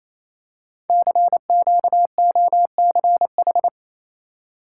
Morse code, heard as a keyed tone, about 28 wpm, 700 Hz. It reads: CQOCH
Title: Code Groups - 5 Letter:Numbers - 28wpm